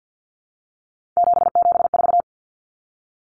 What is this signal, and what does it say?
Morse code, heard as a keyed tone, about 50 wpm, 700 Hz. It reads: 774
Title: Common Cut Numbers - 3-Digit 50wpm